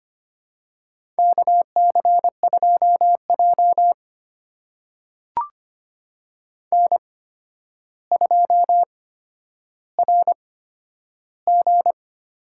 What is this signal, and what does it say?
Morse code, heard as a keyed tone, about 25 wpm, spaced as for 11 wpm, 700 Hz. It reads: KC2J N2RG
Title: USA Call Signs - 8x Character Spacing - 25wpm